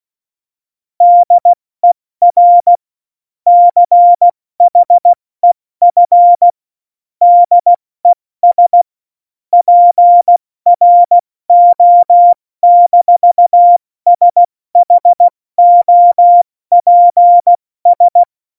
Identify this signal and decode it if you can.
Morse code, heard as a keyed tone, about 16 wpm, 700 Hz. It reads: DER CHEF DES PRO-SHOPS
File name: Morsekurs / Witze / Witz 24. November 2020, 70 F80